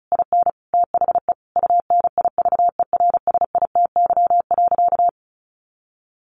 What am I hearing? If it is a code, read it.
Morse code, heard as a keyed tone, about 34 wpm, 700 Hz. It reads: IN THE UNIVERSITY.
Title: Sentences from Top 1300 Words 35wpm